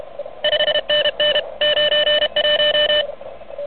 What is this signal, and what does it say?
Morse code, heard as a keyed tone, about 33 wpm, 3000 Hz, with noise. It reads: 5NN 91
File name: truSDX